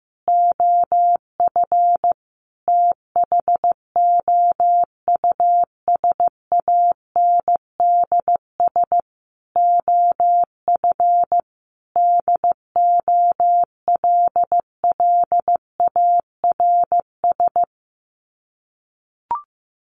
Morse code, heard as a keyed tone, about 14 wpm, 700 Hz. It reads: OF THOUSANDS OF DOLLARS
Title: Sets of 5 Words Encore 15wpm